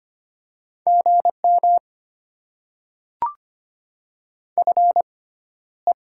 Morse code, heard as a keyed tone, about 25 wpm, spaced as for 13 wpm, 700 Hz. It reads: GM FE
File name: Top 20 QSO Elements - 6x Character Spacing - 25wpm